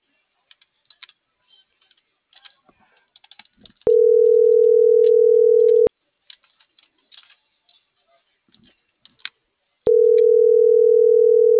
Ringback tone (North America), 440 + 480 Hz.